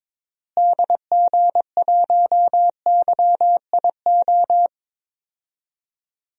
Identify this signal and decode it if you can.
Morse code, heard as a keyed tone, about 22 wpm, 700 Hz. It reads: DG1YIO
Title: DXCC-100 Call Signs - 6 Characters - Repeated 2x - 22wpm